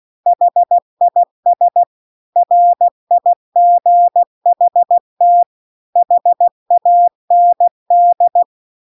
Morse code, heard as a keyed tone, about 16 wpm, 700 Hz. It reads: HIS RIGHT HAND
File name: CW-The-Big-Twoheaded-River-2-16WPM